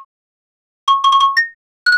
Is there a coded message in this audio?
No.